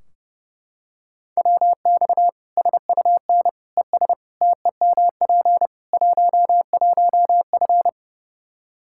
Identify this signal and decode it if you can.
Morse code, heard as a keyed tone, about 30 wpm, 700 Hz. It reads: WX SUN ES TEMP 11F
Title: Practice QSO - Exchange 2 - Info - 30wpm